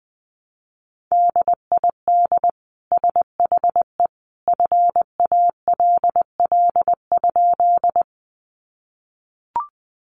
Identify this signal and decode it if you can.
Morse code, heard as a keyed tone, about 20 wpm, 700 Hz. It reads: DID SHE FALL?
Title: Sentences from Top 500 Words 20wpm